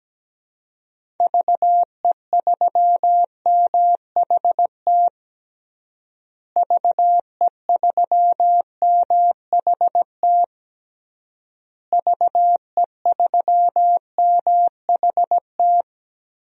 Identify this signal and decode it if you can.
Morse code, heard as a keyed tone, about 17 wpm, 700 Hz. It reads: VE3MHT VE3MHT VE3MHT
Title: DXCC-100 Call Signs - 6 Characters - Repeated 3x - 17wpm